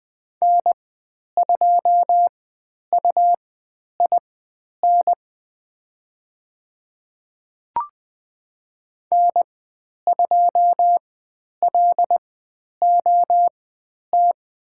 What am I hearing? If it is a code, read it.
Morse code, heard as a keyed tone, about 20 wpm, spaced as for 10 wpm, 700 Hz. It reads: N2UIN N2LOT
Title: 109 Lesson 18g - Letter m - Call Signs 20wpm